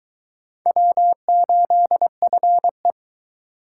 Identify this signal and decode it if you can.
Morse code, heard as a keyed tone, about 22 wpm, 700 Hz. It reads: W8FE